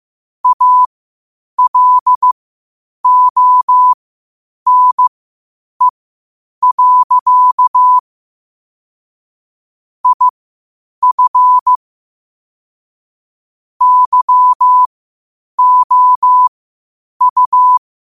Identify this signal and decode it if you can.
Morse code, heard as a keyed tone, about 15 wpm, spaced as for 8 wpm, 1000 Hz. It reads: ALONE. IF YOU
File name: qotdfiles / 8wpm